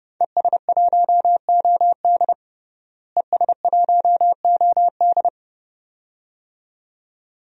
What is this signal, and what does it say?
Morse code, heard as a keyed tone, about 30 wpm, 700 Hz. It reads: ES1OD ES1OD
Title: DXCC-100 Call Signs - 5 Characters - Repeated 2x - 30wpm